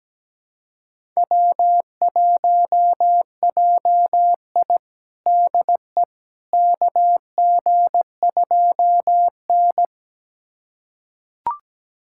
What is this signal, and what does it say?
Morse code, heard as a keyed tone, about 17 wpm, 700 Hz. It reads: W1JI DE KG2N
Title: USA Callsign DE Callsign - 4 Characters - 17wpm